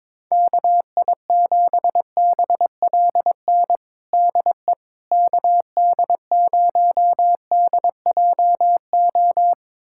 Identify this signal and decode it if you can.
Morse code, heard as a keyed tone, about 22 wpm, 700 Hz. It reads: KI7BLN DE KD0DJO
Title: USA Callsign DE Callsign 22wpm